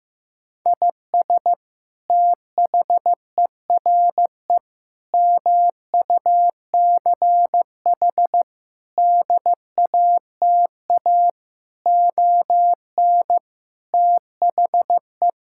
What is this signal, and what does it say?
Morse code, heard as a keyed tone, about 16 wpm, 700 Hz. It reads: IS THERE MUCH DATA ON THE
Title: Sentences from Top 1000 Words - Review Part 6 - 15wpm